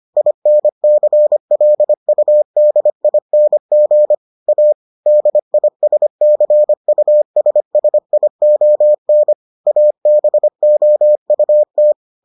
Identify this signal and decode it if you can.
Morse code, heard as a keyed tone, about 25 wpm, 600 Hz. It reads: INCLUDING A DISCUSSION ABOUT